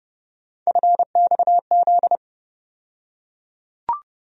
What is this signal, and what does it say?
Morse code, heard as a keyed tone, about 30 wpm, 700 Hz. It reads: FXZ
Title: USA Call Sign Postfixes - 3 Letters 30wpm